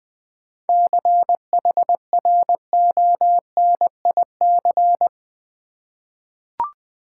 Morse code, heard as a keyed tone, about 20 wpm, 690 Hz. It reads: CHRONIC
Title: 7 Letter Words - First 3 Letters Given - 20wpm